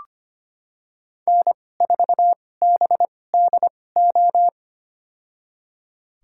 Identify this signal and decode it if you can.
Morse code, heard as a keyed tone, about 25 wpm, spaced as for 18 wpm, 700 Hz. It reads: N4BDO
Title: USA Call Signs - 5 Characters - 2x Character Spacing - 25wpm